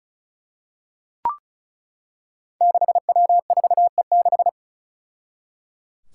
Morse code, heard as a keyed tone, about 34 wpm, 700 Hz. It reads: 6W4E6
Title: Five Challenging Letter-Number Combinations 35wpm